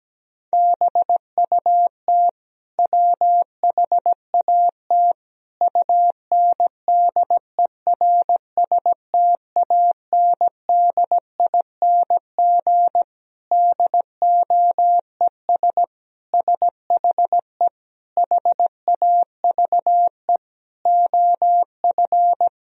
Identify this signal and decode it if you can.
Morse code, heard as a keyed tone, about 17 wpm, 700 Hz. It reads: BUT WHAT UNDERSTANDING DOES SHE HAVE OF
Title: Sentences from Top 1000 Words - Review Part 3 - 17wpm